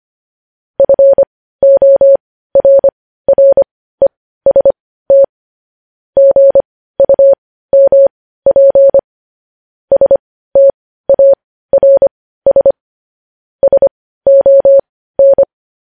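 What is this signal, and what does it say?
Morse code, heard as a keyed tone, about 25 wpm, spaced as for 15 wpm, 550 Hz. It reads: FORREST GUMP STARS SON
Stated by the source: News headlines sent as Morse practice at 15.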